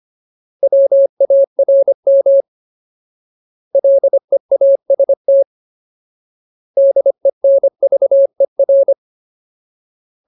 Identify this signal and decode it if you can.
Morse code, heard as a keyed tone, about 25 wpm, 550 Hz. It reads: WARM LEAST DENVER
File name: Ultimate Word List / 25WPM Ultimate